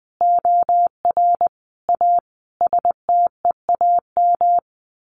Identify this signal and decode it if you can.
Morse code, heard as a keyed tone, about 20 wpm, 700 Hz. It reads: OR A STEAM
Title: A Sound Of Thunder 20wpm